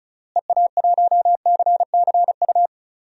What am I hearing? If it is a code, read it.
Morse code, heard as a keyed tone, about 36 wpm, 700 Hz. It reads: EA1CCU
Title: DXCC-100 Call Signs - 6 Characters - Repeated 3x - 35wpm